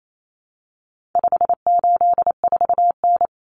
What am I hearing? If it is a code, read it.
Morse code, heard as a keyed tone, about 28 wpm, 700 Hz. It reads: 584N
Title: Common Cut Numbers - 4-Digit 28wpm